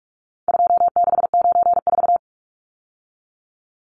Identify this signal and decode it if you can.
Morse code, heard as a keyed tone, about 44 wpm, 700 Hz. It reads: 2694
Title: Numbers - 4-Digit 45wpm